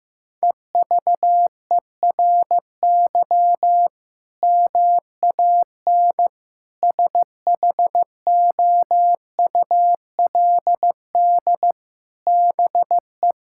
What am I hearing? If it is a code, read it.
Morse code, heard as a keyed tone, about 16 wpm, 700 Hz. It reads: EVERY MAN SHOULD BE
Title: Sentences from Top 1000 Words - Review Part 6 - 15wpm